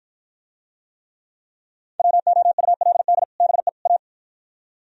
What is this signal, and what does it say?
Morse code, heard as a keyed tone, about 52 wpm, 700 Hz. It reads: WOULD BE A